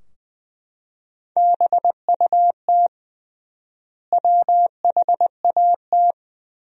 Morse code, heard as a keyed tone, about 20 wpm, 700 Hz. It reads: BUT WHAT
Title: 3x Word-Spacing - Sentences from Top 300 Words - 20wpm